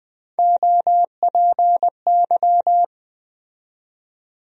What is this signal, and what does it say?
Morse code, heard as a keyed tone, about 20 wpm, 700 Hz. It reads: OPY
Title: USA Call Sign Postfixes - 3 Letters 20wpm